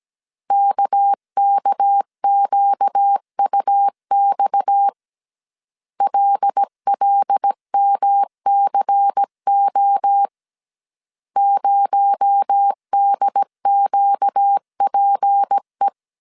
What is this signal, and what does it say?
Morse code, heard as a keyed tone, about 17 wpm, 800 Hz. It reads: KKQUX LLMCO 0DQPE
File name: CW / DET / 1 6 1500C S C 16wpm 2733